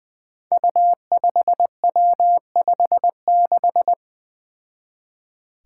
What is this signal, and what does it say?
Morse code, heard as a keyed tone, about 20 wpm, 700 Hz. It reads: U5W56
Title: Five Challenging Letter-Number Combinations 20wpm